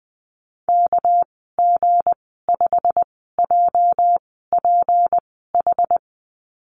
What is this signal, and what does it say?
Morse code, heard as a keyed tone, about 20 wpm, spaced as for 14 wpm, 700 Hz. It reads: KG5JPH
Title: 207 Lesson 33g - Letter j - Call Signs 20wpm